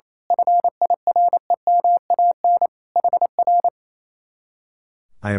Morse code, heard as a keyed tone, about 26 wpm, 700 Hz. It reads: FIREMAN HR
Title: Speed-Racing Common QSO Phrases Encore - 28wpm